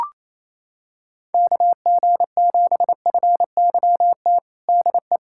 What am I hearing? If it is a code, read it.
Morse code, heard as a keyed tone, about 28 wpm, 700 Hz. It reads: KG7FYT DE
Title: USA Callsign DE Callsign - 6 Characters - 28wpm